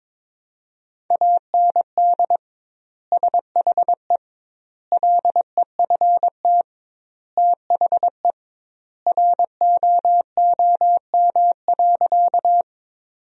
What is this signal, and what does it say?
Morse code, heard as a keyed tone, about 22 wpm, 700 Hz. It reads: AND SHE LEFT THE ROOM.
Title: 2x Word-Spacing - Sentences from Top 300 Words - 22wpm